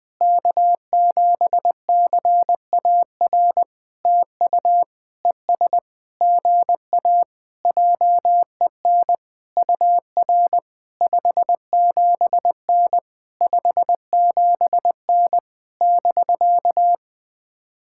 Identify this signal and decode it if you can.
Morse code, heard as a keyed tone, about 20 wpm, 700 Hz. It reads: K7CAR TU ES GA JEN UR 57N 57N <BK>
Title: POTA QSO Practice - 20wpm